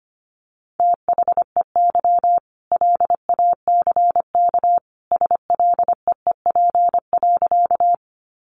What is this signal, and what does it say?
Morse code, heard as a keyed tone, about 25 wpm, 700 Hz. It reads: THEY LACK SLEEP.
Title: Sentences from Top 1100 Words 25wpm